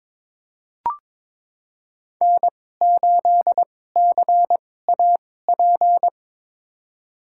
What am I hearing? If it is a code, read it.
Morse code, heard as a keyed tone, about 22 wpm, spaced as for 16 wpm, 700 Hz. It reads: N8CAP
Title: USA Call Signs - 5 Characters - 2x Character Spacing - 22wpm